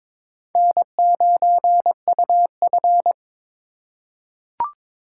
Morse code, heard as a keyed tone, about 22 wpm, 700 Hz. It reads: N9UF
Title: USA Call Signs - 6x Character Spacing - 22wpm